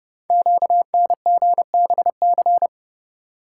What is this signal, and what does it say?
Morse code, heard as a keyed tone, about 30 wpm, 700 Hz. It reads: QNGBC
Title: Code Groups - 5 Letters - 30wpm